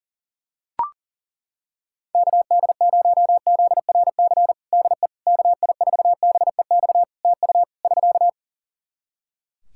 Morse code, heard as a keyed tone, about 37 wpm, 700 Hz. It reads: KD0ZRC DE KI4BEX TU <SK>
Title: Practice QSO - Exchange 4 - Ending - 40wpm